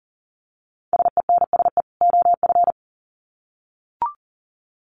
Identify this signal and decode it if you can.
Morse code, heard as a keyed tone, about 40 wpm, 700 Hz. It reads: SENSE OF
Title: Sets of 2 Words 40wpm